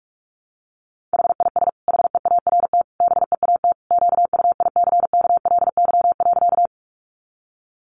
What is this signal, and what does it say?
Morse code, heard as a keyed tone, about 44 wpm, 700 Hz. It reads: HIS HEART BEAT QUICKLY.
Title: Sentences from Top 1300 Words 45wpm